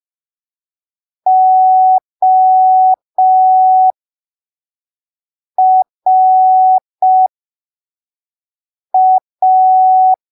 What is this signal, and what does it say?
Morse code, heard as a keyed tone, about 5 wpm, spaced as for 3 wpm, 750 Hz. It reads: ORA
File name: arquivos / texto / texto 26 05ppm